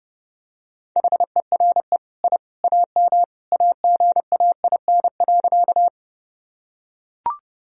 Morse code, heard as a keyed tone, about 30 wpm, 700 Hz. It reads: HERE I AM AGAIN.
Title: Sentences from Top 1000 Words - Review Part 2 - 30wpm